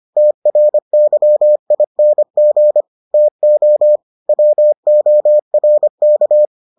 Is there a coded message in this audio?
Yes: Morse code from 0.2 to 6.5 s, heard as a keyed tone, about 25 wpm, 600 Hz. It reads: TRYING TO WORK